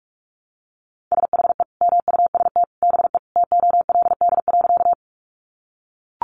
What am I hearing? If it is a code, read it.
Morse code, heard as a keyed tone, about 44 wpm, 700 Hz. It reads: SHE MUST BE TOLD.
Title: Sentences from Top 400 Words 45wpm